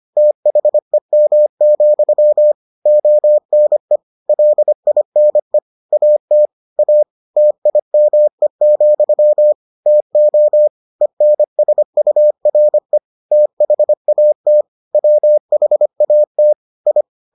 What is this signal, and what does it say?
Morse code, heard as a keyed tone, about 25 wpm, 600 Hz. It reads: THEM, ONE LINE AT A TIME, TO ENSURE THAT WHAT I